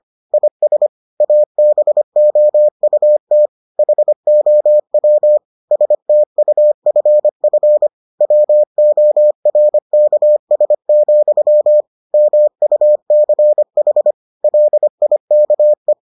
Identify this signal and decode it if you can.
Morse code, heard as a keyed tone, about 25 wpm, 600 Hz. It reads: IS ABOUT HOW STUFF WORKS, MUCH LIKE